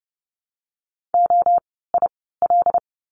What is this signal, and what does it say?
Morse code, heard as a keyed tone, about 30 wpm, spaced as for 17 wpm, 700 Hz. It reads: OIL
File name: Lesson 16f - Letter c - Words 30wpm